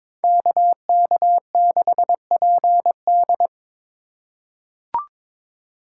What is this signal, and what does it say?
Morse code, heard as a keyed tone, about 22 wpm, 700 Hz. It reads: KK6PD
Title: USA Call Signs - 5 Characters - 2x Character Spacing - 22wpm